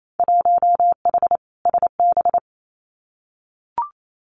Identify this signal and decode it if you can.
Morse code, heard as a keyed tone, about 28 wpm, 700 Hz. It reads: 1H SB
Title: Winter Field Day Exchanges 28wpm